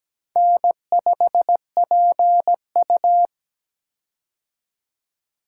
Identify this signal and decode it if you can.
Morse code, heard as a keyed tone, about 17 wpm, 700 Hz. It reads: N5PU
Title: K1USN SST QSO Practice - 17wpm